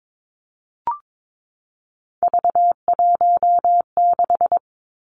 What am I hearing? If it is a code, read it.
Morse code, heard as a keyed tone, about 22 wpm, 700 Hz. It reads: V16
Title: Three Challenging Letter-Number Combinations 22wpm